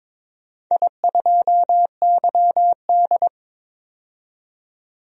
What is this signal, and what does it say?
Morse code, heard as a keyed tone, about 22 wpm, 700 Hz. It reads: I2YD
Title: DXCC-100 Call Signs - 4 Characters - 22wpm